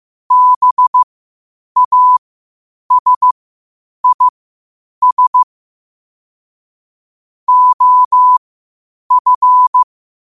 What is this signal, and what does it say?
Morse code, heard as a keyed tone, about 15 wpm, spaced as for 8 wpm, 1000 Hz. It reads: BASIS OF